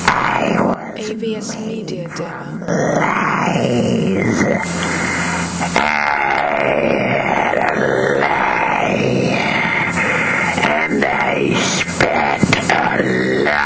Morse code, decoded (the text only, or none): none